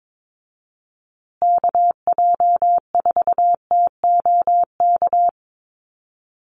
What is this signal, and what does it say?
Morse code, heard as a keyed tone, about 22 wpm, 700 Hz. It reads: KJ4TOK
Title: USA Call Signs Repeated 3x 22wpm